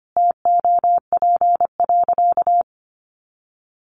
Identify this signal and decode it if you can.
Morse code, heard as a keyed tone, about 25 wpm, 700 Hz. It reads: TOP.
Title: Sentences from Top 900 Words 25wpm